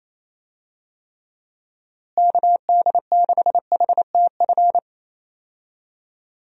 Morse code, heard as a keyed tone, about 28 wpm, 700 Hz. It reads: KD6HTF
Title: USA Call Signs - 6 Characters - Repeated 2x - 28wpm